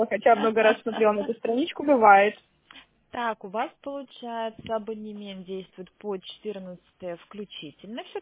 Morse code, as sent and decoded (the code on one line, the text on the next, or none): none